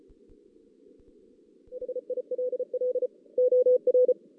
Morse code, heard as a keyed tone, about 34 wpm, 500 Hz. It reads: HILL OR
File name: 2016 JAN CW NAQP